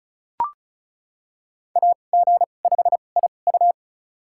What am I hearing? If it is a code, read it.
Morse code, heard as a keyed tone, about 36 wpm, spaced as for 25 wpm, 700 Hz. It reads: AG5IU
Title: USA Call Signs - 5 Characters - 2x Character Spacing - 35wpm